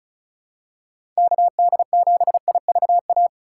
Morse code, heard as a keyed tone, about 34 wpm, 700 Hz. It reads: KD7IVA
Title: USA Call Signs - 2x Character Spacing - 35wpm